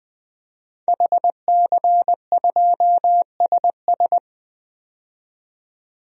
Morse code, heard as a keyed tone, about 20 wpm, 700 Hz. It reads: HC2SS
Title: DXCC-100 Call Signs - 5 Characters - 20wpm